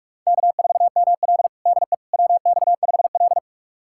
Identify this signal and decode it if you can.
Morse code, heard as a keyed tone, about 46 wpm, 700 Hz. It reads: K4ML DE WX5L
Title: USA Callsign DE Callsign - 4 Characters - 45wpm